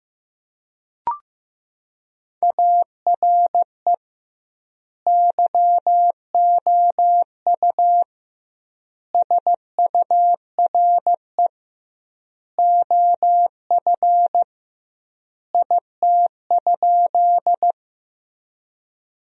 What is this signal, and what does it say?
Morse code, heard as a keyed tone, about 15 wpm, 700 Hz. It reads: ARE YOU SURE OF IT?
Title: 2x Word-Spacing - Sentences from Top 300 Words - 15wpm